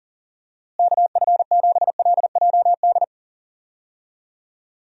Morse code, heard as a keyed tone, about 37 wpm, 700 Hz. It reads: KF7LJD
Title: USA Call Signs - 6 Characters - Repeated 2x - 40wpm